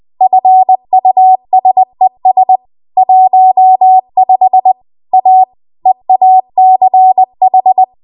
Morse code, heard as a keyed tone, about 20 wpm, 750 Hz. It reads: FUSES 15 A EACH
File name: Morse / 20WPM